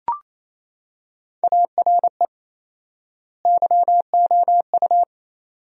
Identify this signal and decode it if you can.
Morse code, heard as a keyed tone, about 28 wpm, 700 Hz. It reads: ARE YOU
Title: 4x Word-Spacing - Sentences from Top 500 Words - 28wpm